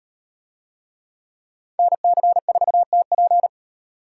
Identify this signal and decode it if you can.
Morse code, heard as a keyed tone, about 38 wpm, 700 Hz. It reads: NC4TP